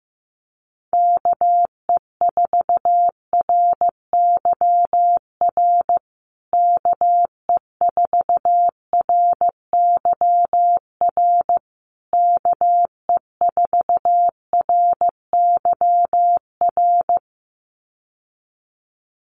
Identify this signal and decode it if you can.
Morse code, heard as a keyed tone, about 15 wpm, 700 Hz. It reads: KE4RYR KE4RYR KE4RYR
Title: USA Call Signs Repeated 3x 15wpm